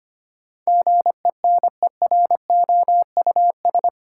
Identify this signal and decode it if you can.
Morse code, heard as a keyed tone, about 25 wpm, 690 Hz. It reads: GENEROUS
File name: 8 Letter Words - First 3 Letters Given - 25wpm